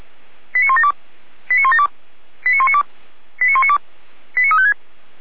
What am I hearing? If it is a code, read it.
Five-tone selective call (ZVEI), heard as quick runs of five tones, tones of about 70 ms. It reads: 89172 89172 89192 89192 89356